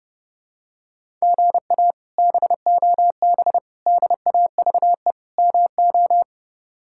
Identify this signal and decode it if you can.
Morse code, heard as a keyed tone, about 30 wpm, 700 Hz. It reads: GA BOB DAVE MO
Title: K1USN SST QSO Practice - 30wpm